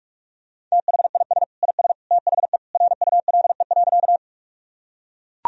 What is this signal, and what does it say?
Morse code, heard as a keyed tone, about 44 wpm, 700 Hz. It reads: THIS IS THE RULE.
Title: Sentences from Top 1000 Words - Review Part 2 - 45wpm